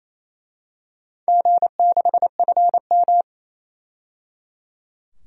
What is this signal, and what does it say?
Morse code, heard as a keyed tone, about 28 wpm, 700 Hz. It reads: G6FM
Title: DXCC-100 Call Signs - 4 Characters - 28wpm